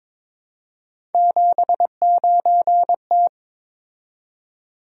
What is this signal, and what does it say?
Morse code, heard as a keyed tone, about 22 wpm, 700 Hz. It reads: 79T
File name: practice / 22wpm / Three Letter-Number Combinations - 22wpm